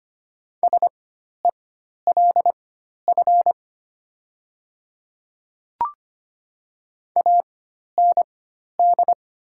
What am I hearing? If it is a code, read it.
Morse code, heard as a keyed tone, about 25 wpm, spaced as for 12 wpm, 690 Hz. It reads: SELF AND
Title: 4x Character Spacing - Top 200 Words - 25wpm